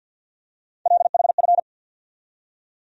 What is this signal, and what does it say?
Morse code, heard as a keyed tone, about 49 wpm, 700 Hz. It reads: LHF
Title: USA Call Sign Postfixes - 3 Letters 50wpm